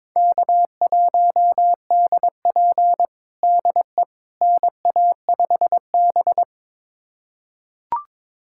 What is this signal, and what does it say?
Morse code, heard as a keyed tone, about 22 wpm, 700 Hz. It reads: K1DP DE NA5B
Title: USA Callsign DE Callsign - 4 Characters - 22wpm